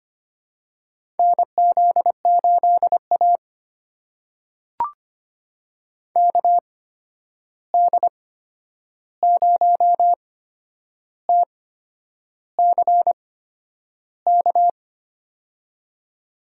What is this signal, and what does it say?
Morse code, heard as a keyed tone, about 25 wpm, spaced as for 9 wpm, 700 Hz. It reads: NZ8A KD0TCK
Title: USA Call Signs - 8x Character Spacing - 25wpm